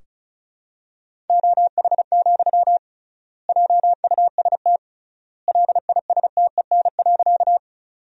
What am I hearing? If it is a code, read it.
Morse code, heard as a keyed tone, about 36 wpm, 700 Hz. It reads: OH, JUST LISTEN.
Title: 3x Word-Spacing - Sentences from Top 500 Words - 35wpm